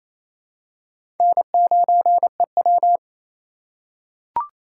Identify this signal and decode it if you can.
Morse code, heard as a keyed tone, about 27 wpm, 700 Hz. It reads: N9EW
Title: USA Call Signs - 2x Character Spacing - 28wpm